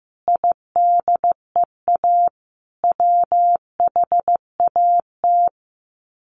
Morse code, heard as a keyed tone, about 15 wpm, 700 Hz. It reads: IDEA WHAT
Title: Sets of 5 Words Encore 15wpm